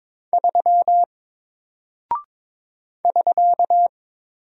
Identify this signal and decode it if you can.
Morse code, heard as a keyed tone, about 22 wpm, 700 Hz. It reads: 3 <SK>
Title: Single Character (Letter:Number:Punctuation:Prosign) 22wpm